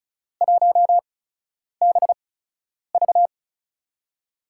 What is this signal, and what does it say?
Morse code, heard as a keyed tone, about 34 wpm, spaced as for 10 wpm, 700 Hz. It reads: 1BV
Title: USA Call Signs - 5 Characters - 8x Character Spacing - 35wpm